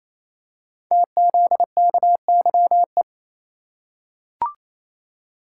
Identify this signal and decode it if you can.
Morse code, heard as a keyed tone, about 28 wpm, 700 Hz. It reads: TZKYE
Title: Code Groups - 5 Letters - 28wpm